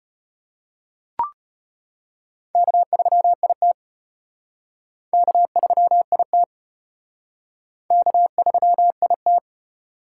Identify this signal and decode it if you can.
Morse code, heard as a keyed tone, about 32 wpm, 700 Hz. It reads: K3IT K3IT K3IT